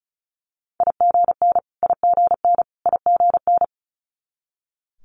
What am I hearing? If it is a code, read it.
Morse code, heard as a keyed tone, about 36 wpm, 700 Hz. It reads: IGN IGN IGN
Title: Multi-letter Phonemes Repeated 3x 35wpm